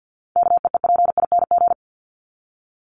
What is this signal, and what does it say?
Morse code, heard as a keyed tone, about 50 wpm, 700 Hz. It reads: KEEPING
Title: ING Suffix - Top 100 Words with Base Words 50wpm